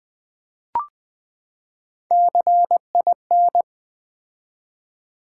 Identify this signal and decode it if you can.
Morse code, heard as a keyed tone, about 20 wpm, 700 Hz. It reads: CIN
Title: Build A Word - 3 Letters At A Time - 6-Letter Words - 20wpm